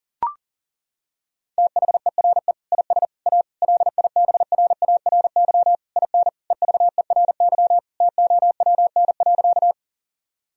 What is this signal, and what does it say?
Morse code, heard as a keyed tone, about 40 wpm, 700 Hz. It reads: THERE IS A LIBRARY IN EVERY TOWN.
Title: Sentences from Top 1000 Words - Review Part 4 - 40wpm